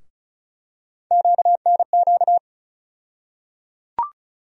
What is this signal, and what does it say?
Morse code, heard as a keyed tone, about 34 wpm, 700 Hz. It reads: QNQ
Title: USA Call Sign Postfixes - 3 Letters 35wpm